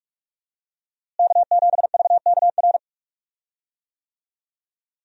Morse code, heard as a keyed tone, about 44 wpm, 700 Hz. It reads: K7VKR
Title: USA Call Signs - 5 Characters - Repeated 2x - 45wpm